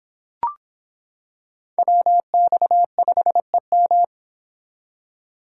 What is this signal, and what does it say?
Morse code, heard as a keyed tone, about 25 wpm, 700 Hz. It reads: WX5EM